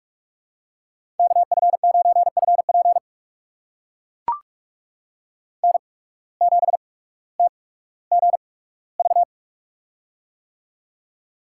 Morse code, heard as a keyed tone, about 42 wpm, spaced as for 19 wpm, 700 Hz. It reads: KF9FP N7TGV
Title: USA Call Signs - 5 Characters - 8x Character Spacing - 45wpm